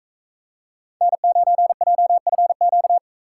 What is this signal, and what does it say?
Morse code, heard as a keyed tone, about 34 wpm, 700 Hz. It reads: N9JFQ
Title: Speed Racing - USA 1x3 Call Signs 28wpm